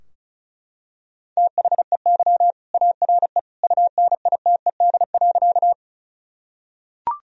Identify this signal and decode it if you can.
Morse code, heard as a keyed tone, about 34 wpm, 700 Hz. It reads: THEY ARE UNITED.
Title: Sentences from Top 1000 Words - Review Part 4 - 35wpm